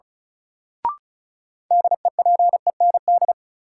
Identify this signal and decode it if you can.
Morse code, heard as a keyed tone, about 36 wpm, 700 Hz. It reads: DEPEND
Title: Sentences from Top 1300 Words - New Word Drill - 35wpm